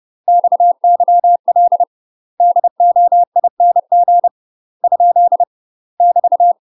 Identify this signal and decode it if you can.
Morse code, heard as a keyed tone, about 30 wpm, 700 Hz. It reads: XYL DOING ? <BT>
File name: UK MASTER QSOs TRAINING / 30WPM / UK MASTER QSOs 0048